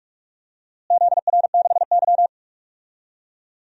Morse code, heard as a keyed tone, about 43 wpm, 690 Hz. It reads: ZR6Y
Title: DXCC-100 Call Signs - 4 Characters - Repeated 3x - 45wpm